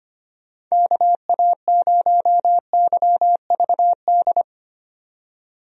Morse code, heard as a keyed tone, about 25 wpm, 700 Hz. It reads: KA0YVD
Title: USA Call Signs - 6 Characters - 25wpm